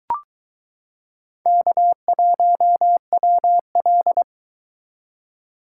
Morse code, heard as a keyed tone, about 23 wpm, 700 Hz. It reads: K1WL